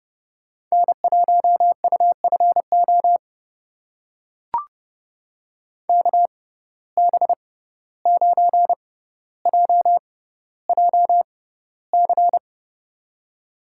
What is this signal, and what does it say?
Morse code, heard as a keyed tone, about 30 wpm, spaced as for 15 wpm, 700 Hz. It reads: N1UFO KB9JJC
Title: USA Call Signs - 6x Character Spacing - 30wpm